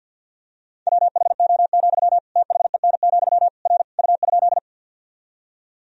Morse code, heard as a keyed tone, about 53 wpm, 700 Hz. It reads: WHO, THEN, R U?
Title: Sentences from Top 100 Words and Abbreviations 50wpm